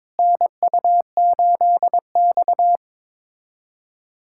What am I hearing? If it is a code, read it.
Morse code, heard as a keyed tone, about 22 wpm, 700 Hz. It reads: NU8X
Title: USA Callsign DE Callsign - 4 Characters - 22wpm